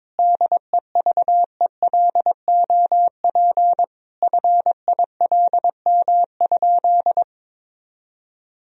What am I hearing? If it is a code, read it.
Morse code, heard as a keyed tone, about 22 wpm, 700 Hz. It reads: DEVELOP FILM?
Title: Sentences from Top 1000 Words - Review Part 3 - 22wpm